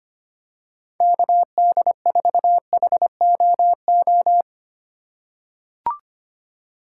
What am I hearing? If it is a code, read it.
Morse code, heard as a keyed tone, about 25 wpm, 700 Hz. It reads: KD4HOO
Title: USA Call Signs - 6 Characters - 2x Character Spacing - 25wpm